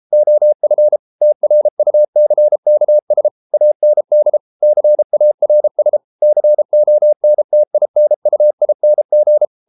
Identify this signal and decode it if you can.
Morse code, heard as a keyed tone, about 33 wpm, 600 Hz. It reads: OF TRUCKS AND CARS CONTINUING